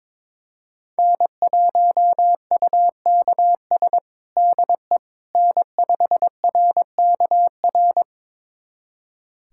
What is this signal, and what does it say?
Morse code, heard as a keyed tone, about 22 wpm, 700 Hz. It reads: N1UKS DE N5RKR
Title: USA Callsign DE Callsign - 5 Characters - 22wpm